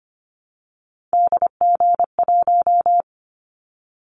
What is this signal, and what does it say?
Morse code, heard as a keyed tone, about 25 wpm, 700 Hz. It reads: DG1
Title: Callsign Prefixes for Top 25 DXCC - 25wpm